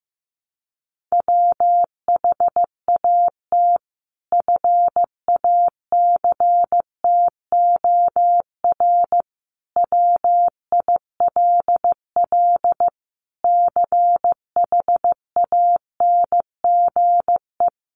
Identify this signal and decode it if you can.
Morse code, heard as a keyed tone, about 15 wpm, 700 Hz. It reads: WHAT FACTOR WILL CHANGE
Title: Sentences from Top 1300 Words 15wpm